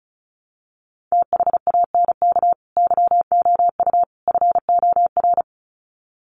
Morse code, heard as a keyed tone, about 34 wpm, 700 Hz. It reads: THANK YOU FOR
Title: Sets of 3 Words 35wpm